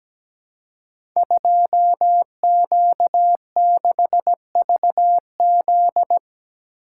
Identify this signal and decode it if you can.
Morse code, heard as a keyed tone, about 17 wpm, 700 Hz. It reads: 2Q6VZ
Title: Code Groups - 5 Letter:Numbers - 17wpm